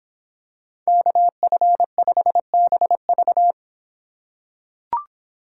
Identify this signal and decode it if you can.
Morse code, heard as a keyed tone, about 25 wpm, 700 Hz. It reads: KF5BV